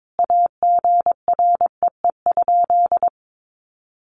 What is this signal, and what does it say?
Morse code, heard as a keyed tone, about 22 wpm, 700 Hz. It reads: AGREE?
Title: Sentences from Top 1100 Words 22wpm